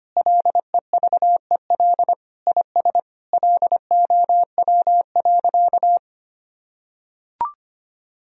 Morse code, heard as a keyed tone, about 25 wpm, 700 Hz. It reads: LEVEL IS LOW.
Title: Sentences from Top 1000 Words - Review Part 4 - 25wpm